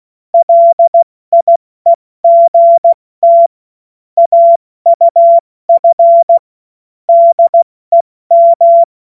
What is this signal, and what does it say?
Morse code, heard as a keyed tone, about 16 wpm, 680 Hz. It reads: LIEGT AUF DEM
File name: Morsekurs / Witze / Witz 06. September 2022 70 F80